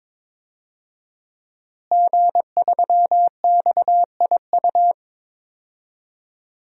Morse code, heard as a keyed tone, about 22 wpm, 700 Hz. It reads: G3XIU
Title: DXCC-100 Call Signs - 5 Characters - Repeated 2x - 22wpm